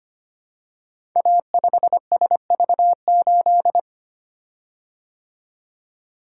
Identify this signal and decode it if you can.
Morse code, heard as a keyed tone, about 25 wpm, 700 Hz. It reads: A5SV8
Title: Code Groups - 5 Letter:Numbers - 25wpm